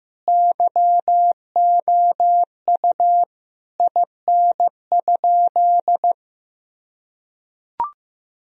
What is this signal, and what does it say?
Morse code, heard as a keyed tone, about 15 wpm, 700 Hz. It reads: YOU IN?
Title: Sentences from Top 1000 Words - Review Part 1 - 15wpm